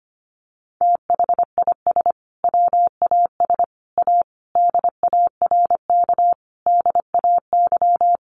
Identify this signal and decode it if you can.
Morse code, heard as a keyed tone, about 25 wpm, 700 Hz. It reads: THIS WAS A DARK DAY
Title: Sentences from Top 1300 Words 25wpm